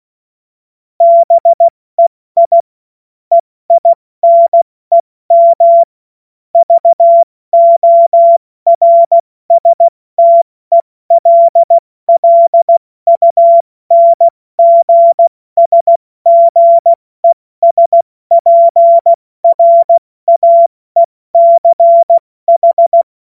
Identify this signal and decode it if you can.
Morse code, heard as a keyed tone, about 16 wpm, 680 Hz. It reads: BEI EINEM VORSTELLUNGSGESPRAECH
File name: Morsekurs / Witze / Witz 11. April 70 F80